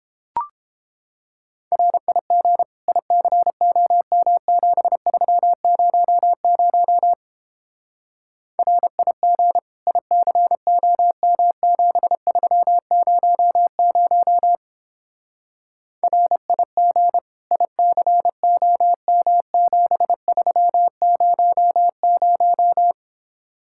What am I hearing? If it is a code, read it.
Morse code, heard as a keyed tone, about 29 wpm, 700 Hz. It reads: RIG ICOM7300 RIG ICOM7300 RIG ICOM7300